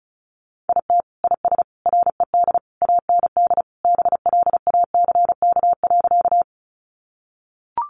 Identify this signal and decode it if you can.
Morse code, heard as a keyed tone, about 34 wpm, 700 Hz. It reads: IT IS RED AND BLACK.
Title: Additional Easy Phrases 35wpm